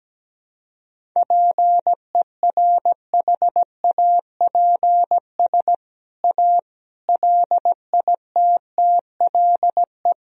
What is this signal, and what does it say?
Morse code, heard as a keyed tone, about 17 wpm, 700 Hz. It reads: PERHAPS A LITTLE
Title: Sentences from Top 1000 Words - Review Part 1 - 17wpm